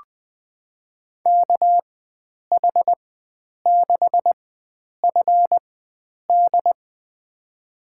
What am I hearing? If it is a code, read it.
Morse code, heard as a keyed tone, about 20 wpm, spaced as for 9 wpm, 700 Hz. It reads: KH6FD
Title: USA Call Signs - 4x Character Spacing - 20wpm